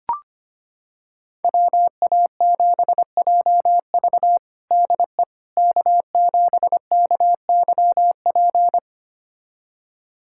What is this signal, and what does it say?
Morse code, heard as a keyed tone, about 25 wpm, 700 Hz. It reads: WA7JV DE K7KYP
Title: USA Callsign DE Callsign - 5 Characters - 25wpm